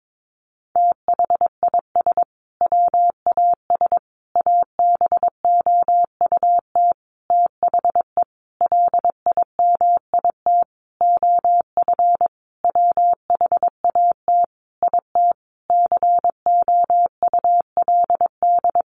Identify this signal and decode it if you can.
Morse code, heard as a keyed tone, about 22 wpm, 700 Hz. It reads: THIS WAS ABOUT THE LIMIT OF WHAT IT COULD
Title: Sentences from Top 1500 Words 22wpm